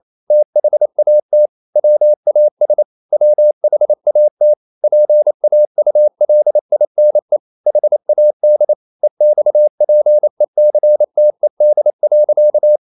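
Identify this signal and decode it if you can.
Morse code, heard as a keyed tone, about 28 wpm, 600 Hz. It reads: THAT WAS WHAT PAULINE HAD EXPECTED.
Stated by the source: Morse code practice at 28 wpm, text of Ken Follett's novel Never in English, from CW med Gnister.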